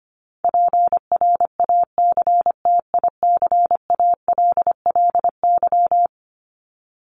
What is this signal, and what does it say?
Morse code, heard as a keyed tone, about 25 wpm, 700 Hz. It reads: PRACTICALLY
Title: LY Suffix - Top 100 Words with Base Words 25wpm